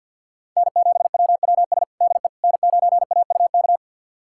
Morse code, heard as a keyed tone, about 49 wpm, 700 Hz. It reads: N7WWS DE N9AUK
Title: USA Callsign DE Callsign - 5 Characters - 50wpm